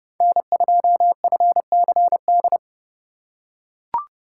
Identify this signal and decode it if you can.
Morse code, heard as a keyed tone, about 28 wpm, 700 Hz. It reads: N2FCD